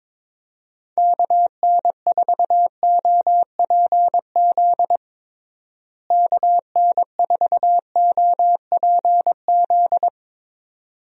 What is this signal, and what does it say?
Morse code, heard as a keyed tone, about 22 wpm, 700 Hz. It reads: KN4OPZ KN4OPZ
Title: USA Call Signs - 6 Characters - Repeated 2x - 22wpm